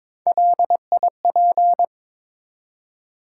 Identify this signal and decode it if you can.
Morse code, heard as a keyed tone, about 22 wpm, 700 Hz. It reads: LIP
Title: Three Letter Words - 22wpm